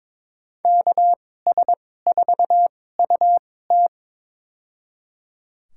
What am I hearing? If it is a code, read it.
Morse code, heard as a keyed tone, about 21 wpm, spaced as for 16 wpm, 700 Hz. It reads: KS4UT
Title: USA Call Signs - 2x Character Spacing - 22wpm